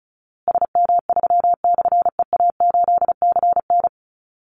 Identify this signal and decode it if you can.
Morse code, heard as a keyed tone, about 34 wpm, 700 Hz. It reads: SM3/EA8CN
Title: Call Signs Repeated 2x - 35wpm